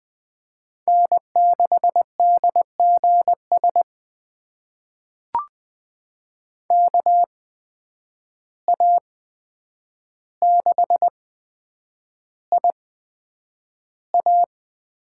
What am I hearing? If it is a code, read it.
Morse code, heard as a keyed tone, about 20 wpm, spaced as for 9 wpm, 700 Hz. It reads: N6DGS KA6IA
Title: USA Call Signs - 8x Character Spacing - 20wpm